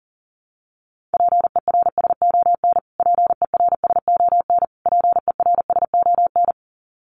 Morse code, heard as a keyed tone, about 38 wpm, 700 Hz. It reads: PERSON PERSON PERSON
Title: Top 500 Words Repeated 40wpm